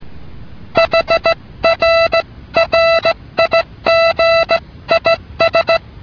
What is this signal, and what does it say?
Morse code, heard as a keyed tone, about 16 wpm, 660 Hz. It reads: HRRIG IS